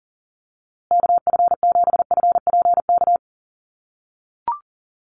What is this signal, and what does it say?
Morse code, heard as a keyed tone, about 40 wpm, 700 Hz. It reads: KF7FPK
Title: USA Call Signs Repeated 2x 40wpm